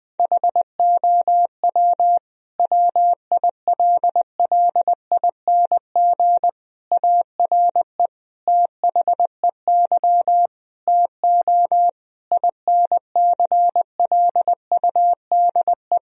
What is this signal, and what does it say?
Morse code, heard as a keyed tone, about 20 wpm, 700 Hz. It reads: HOW WILLING ARE THEY TO INCLUDE
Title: Sentences from Top 1000 Words - Review Part 3 - 20wpm